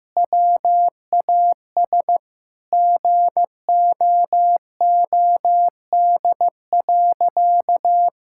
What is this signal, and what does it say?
Morse code, heard as a keyed tone, about 15 wpm, 700 Hz. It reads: WAS GOOD.
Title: Sentences from Top 1000 Words - Review Part 5 - 15wpm